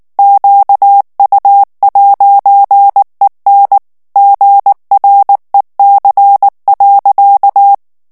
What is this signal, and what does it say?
Morse code, heard as a keyed tone, about 19 wpm, 800 Hz. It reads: QU'EN GREC.
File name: CW / FICHIERS-MP3 / FRANCAIS TEXTE CLAIR 2 VITESSE 18 MOTS MINUTES